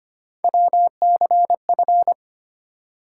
Morse code, heard as a keyed tone, about 25 wpm, 700 Hz. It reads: WCF
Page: morse-code-ninja-practice-25wpm